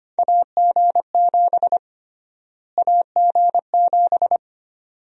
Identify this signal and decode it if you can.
Morse code, heard as a keyed tone, about 25 wpm, 700 Hz. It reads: AG7 AG7
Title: USA Call Sign Prefixes - Repeated 2x - 25wpm